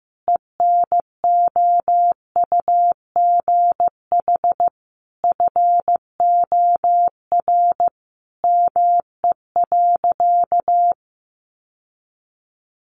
Morse code, heard as a keyed tone, about 15 wpm, 700 Hz. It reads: ENOUGH FOR ME.
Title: Sentences from Top 900 Words 15wpm